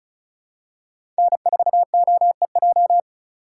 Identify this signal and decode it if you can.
Morse code, heard as a keyed tone, about 36 wpm, 700 Hz. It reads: N4OEJ
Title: USA Call Signs - 5 Characters - 2x Character Spacing - 35wpm